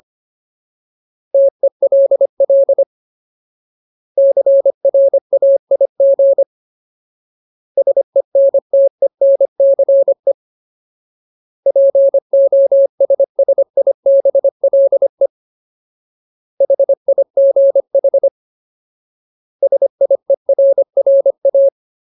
Morse code, heard as a keyed tone, about 25 wpm, 550 Hz. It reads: TELL CRAIG SENTENCE POSSIBLE HIGH SIERRA
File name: Ultimate Word List / 25WPM Ultimate